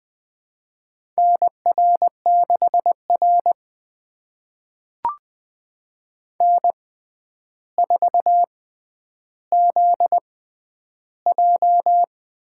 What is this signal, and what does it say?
Morse code, heard as a keyed tone, about 20 wpm, spaced as for 10 wpm, 700 Hz. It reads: NR6R N4ZJ
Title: USA Call Signs - 4 Characters - 6x Character Spacing - 20wpm